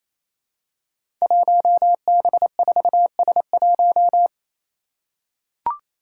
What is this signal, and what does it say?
Morse code, heard as a keyed tone, about 28 wpm, 700 Hz. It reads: 1B4S1
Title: Five Challenging Letter-Number Combinations 28wpm